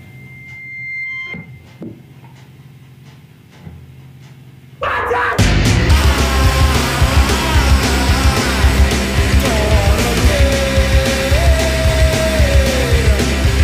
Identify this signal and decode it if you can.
Dial-up modem handshake.